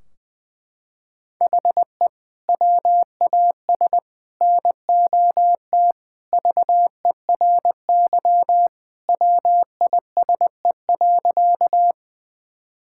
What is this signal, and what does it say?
Morse code, heard as a keyed tone, about 20 wpm, 700 Hz. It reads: HE WAS NOT VERY WISE.
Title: Sentences from Top 1600 Words 20wpm